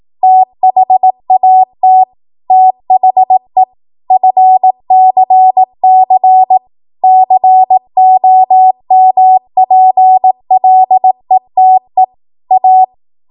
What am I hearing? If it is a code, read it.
Morse code, heard as a keyed tone, about 18 wpm, 750 Hz. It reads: THAT THE FCC COMPLETE A
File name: Morse / 18WPM